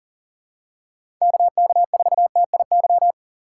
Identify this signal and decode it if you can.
Morse code, heard as a keyed tone, about 40 wpm, 700 Hz. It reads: KK4TIY
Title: USA Call Signs - 4x Character Spacing - 40wpm